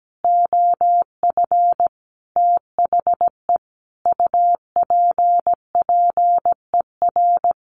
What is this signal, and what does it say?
Morse code, heard as a keyed tone, about 17 wpm, 700 Hz. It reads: OF THE UPPER
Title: A Sound Of Thunder 17wpm